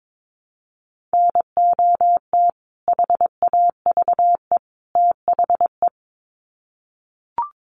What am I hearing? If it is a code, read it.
Morse code, heard as a keyed tone, about 22 wpm, 700 Hz. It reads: NOT HAVE THE
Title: Sets of 3 Words 22wpm